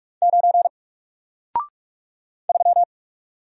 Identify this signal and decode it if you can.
Morse code, heard as a keyed tone, about 44 wpm, 700 Hz. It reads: 9 3
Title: Single Character (Letter:Number:Punctuation:Prosign) Alt 45wpm